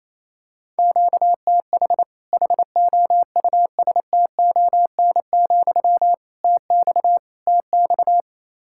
Morse code, heard as a keyed tone, about 28 wpm, 700 Hz. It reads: QTH HOUSTON, TX TX
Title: Practice QSO - Exchange 1 - Beginning - 28wpm